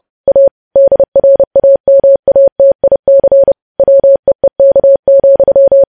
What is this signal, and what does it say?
Morse code, heard as a keyed tone, about 30 wpm, 550 Hz. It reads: A DRAMATIC WEEK,
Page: news headlines in morse code at 30 wpm